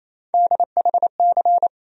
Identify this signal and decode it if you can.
Morse code, heard as a keyed tone, about 28 wpm, 700 Hz. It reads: DHC
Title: USA Call Sign Postfixes - 3 Letters 28wpm